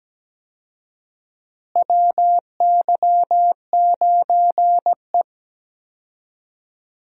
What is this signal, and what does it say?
Morse code, heard as a keyed tone, about 17 wpm, 700 Hz. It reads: WY9E
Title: USA Call Signs - 4 Characters - Repeated 2x - 17wpm